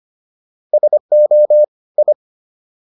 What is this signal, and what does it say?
Morse code, heard as a keyed tone, about 25 wpm, 600 Hz. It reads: SO I